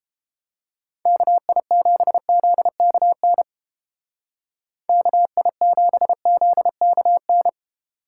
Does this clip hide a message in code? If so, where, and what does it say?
Yes: Morse code from 1.1 to 7.5 s, heard as a keyed tone, about 32 wpm, 700 Hz. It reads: KI7ZKN KI7ZKN